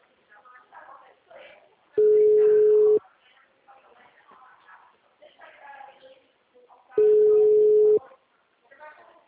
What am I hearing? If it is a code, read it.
Ringback tone (Europe, 425 Hz), with speech.
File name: out- -2017- - - .90143